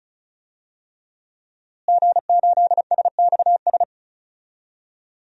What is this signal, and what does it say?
Morse code, heard as a keyed tone, about 34 wpm, 700 Hz. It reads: G8SXS
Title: DXCC-100 Call Signs - 5 Characters - Repeated 3x - 35wpm